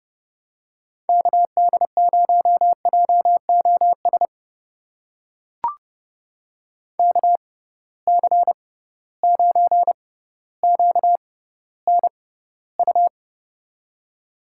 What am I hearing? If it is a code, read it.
Morse code, heard as a keyed tone, about 30 wpm, spaced as for 15 wpm, 700 Hz. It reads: KD0JOS KC9QNU
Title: USA Call Signs - 6 Characters - 6x Character Spacing - 30wpm